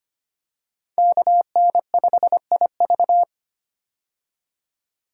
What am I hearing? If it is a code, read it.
Morse code, heard as a keyed tone, about 25 wpm, 700 Hz. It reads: KN5IV
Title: Code Groups - 5 Letter:Numbers - 25wpm